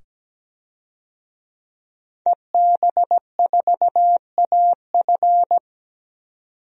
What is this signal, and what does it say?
Morse code, heard as a keyed tone, about 17 wpm, 700 Hz. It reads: EB4AF
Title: DXCC-100 Call Signs - 5 Characters - 17wpm